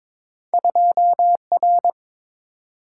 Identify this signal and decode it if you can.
Morse code, heard as a keyed tone, about 22 wpm, 700 Hz. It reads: 2R
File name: Two Letter-Number Combinations - 22wpm